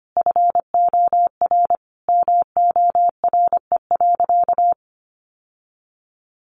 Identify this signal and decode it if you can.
Morse code, heard as a keyed tone, about 25 wpm, 700 Hz. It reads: FOR MORE.
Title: Sentences from Top 1100 Words 25wpm